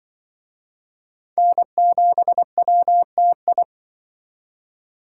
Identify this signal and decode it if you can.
Morse code, heard as a keyed tone, about 24 wpm, 700 Hz. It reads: N7WTI
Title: Speed Racing - USA 1x3 Call Signs 20wpm